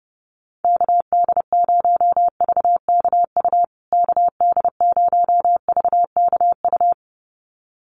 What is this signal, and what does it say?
Morse code, heard as a keyed tone, about 30 wpm, 700 Hz. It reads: KD0VKU KD0VKU
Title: USA Call Signs Repeated 2x 30wpm